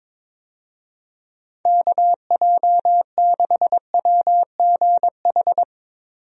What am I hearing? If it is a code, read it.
Morse code, heard as a keyed tone, about 22 wpm, 700 Hz. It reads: KJ6WGH
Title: USA Call Signs - 6 Characters - Repeated 2x - 22wpm